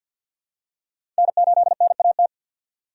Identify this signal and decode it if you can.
Morse code, heard as a keyed tone, about 52 wpm, 700 Hz. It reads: N8NAT
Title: USA Call Signs - 5 Characters - 6x Character Spacing - 50wpm